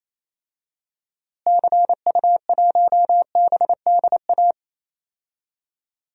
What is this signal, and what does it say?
Morse code, heard as a keyed tone, about 27 wpm, 700 Hz. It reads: CU1BDA
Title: DXCC-100 Call Signs - 6 Characters - Repeated 2x - 28wpm